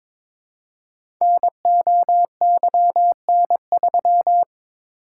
Morse code, heard as a keyed tone, about 22 wpm, 700 Hz. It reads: NOYN3
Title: Code Groups - 5 Letter:Numbers - 22wpm